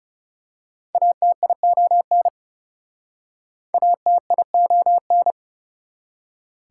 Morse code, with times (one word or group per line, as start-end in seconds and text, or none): 0.9-2.3 ATION
3.7-5.3 ATION